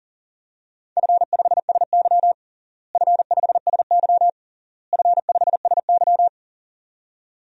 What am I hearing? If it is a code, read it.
Morse code, heard as a keyed tone, about 40 wpm, 700 Hz. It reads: F5SY F5SY F5SY
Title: DXCC-100 Call Signs - 4 Characters - Repeated 3x - 40wpm